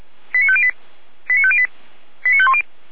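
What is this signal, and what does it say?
Five-tone selective call (ZVEI), heard as quick runs of five tones, tones of about 70 ms. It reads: 89508 89508 89510
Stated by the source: ZVEI five-tone selective call 89108 received by SDR, logged by feuerwehr-orlamuende fire brigade server.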